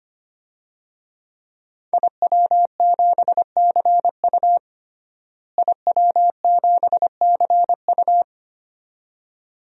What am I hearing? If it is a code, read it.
Morse code, heard as a keyed tone, about 25 wpm, 700 Hz. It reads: IW7CU IW7CU
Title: DXCC-100 Call Signs - 5 Characters - Repeated 2x - 25wpm